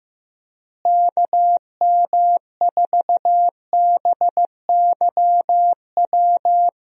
Morse code, heard as a keyed tone, about 15 wpm, 700 Hz. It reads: KM4BYW
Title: USA Call Signs - 6 Characters - Repeated 3x - 15wpm